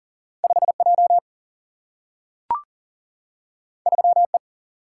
Morse code, heard as a keyed tone, about 40 wpm, 700 Hz. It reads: 5 J 3 E
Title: Single-Letter Single-Number 40wpm